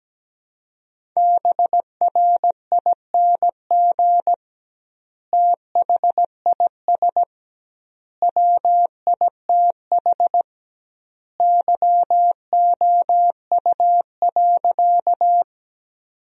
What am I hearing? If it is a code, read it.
Morse code, heard as a keyed tone, about 17 wpm, 700 Hz. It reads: BRING THIS WITH YOU.
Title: 2x Word-Spacing - Sentences from Top 500 Words - 17wpm